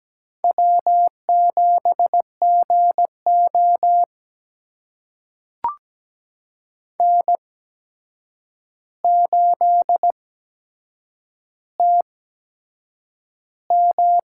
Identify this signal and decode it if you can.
Morse code, heard as a keyed tone, about 17 wpm, spaced as for 7 wpm, 700 Hz. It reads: W7GO N8TM
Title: USA Call Signs - 4 Characters - 8x Character Spacing - 17wpm